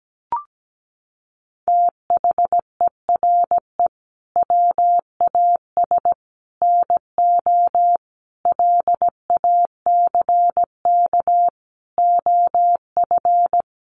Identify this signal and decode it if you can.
Morse code, heard as a keyed tone, about 17 wpm, 700 Hz. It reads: THERE WAS NO LACK OF
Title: Sentences from Top 1100 Words 17wpm